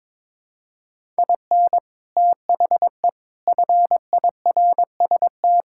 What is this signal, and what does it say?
Morse code, heard as a keyed tone, about 22 wpm, 700 Hz. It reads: IN THE FIRST
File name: practice / 22wpm / Sets of 7 Words - 22wpm